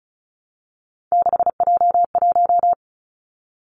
Morse code, heard as a keyed tone, about 36 wpm, 700 Hz. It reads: 6J1
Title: Three Challenging Letter-Number Combinations 35wpm